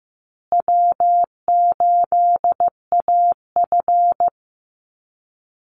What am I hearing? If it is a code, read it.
Morse code, heard as a keyed tone, about 15 wpm, 700 Hz. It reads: W8AF
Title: USA Call Signs Repeated 2x 15wpm